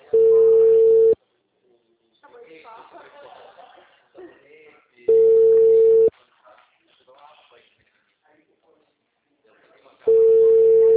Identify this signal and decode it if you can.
Ringback tone (China, 450 Hz), with speech.